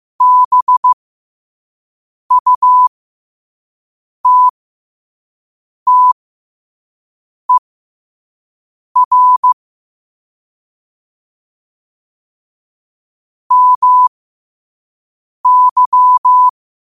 Morse code, heard as a keyed tone, about 15 wpm, spaced as for 5 wpm, 1000 Hz. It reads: BUTTER MY
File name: qotdfiles / 5wpm